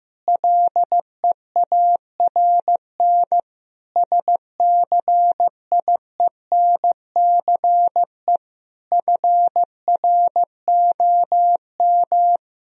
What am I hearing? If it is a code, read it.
Morse code, heard as a keyed tone, about 15 wpm, 700 Hz. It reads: LEARN SCIENCE FROM
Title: Sentences from Top 1000 Words - Review Part 2 - 15wpm